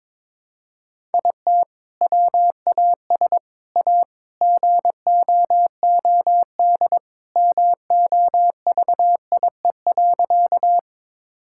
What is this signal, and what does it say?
Morse code, heard as a keyed tone, about 22 wpm, 700 Hz. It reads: IT WAS A GOOD MOVIE.
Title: Sentences from Top 1000 Words - Review Part 5 - 22wpm